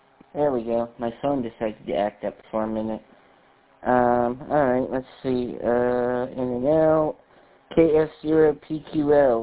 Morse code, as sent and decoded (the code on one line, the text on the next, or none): none